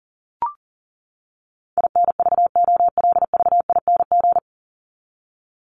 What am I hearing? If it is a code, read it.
Morse code, heard as a keyed tone, about 40 wpm, 700 Hz. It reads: INVOLVING
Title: ING Suffix - Top 100 Words with Base Words 40wpm